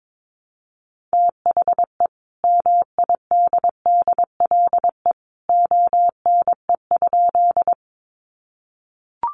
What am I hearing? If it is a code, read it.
Morse code, heard as a keyed tone, about 22 wpm, 700 Hz. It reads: THE MIDDLE ONE?
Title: Sentences from Top 900 Words 22wpm